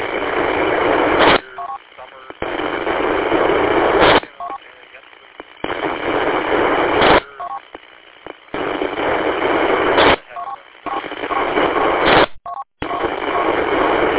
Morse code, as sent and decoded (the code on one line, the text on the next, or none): none